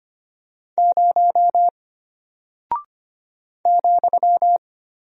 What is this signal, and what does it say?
Morse code, heard as a keyed tone, about 25 wpm, 700 Hz. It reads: 0 ,
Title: Single Character (Letter:Number:Punctuation:Prosign) Alt 25wpm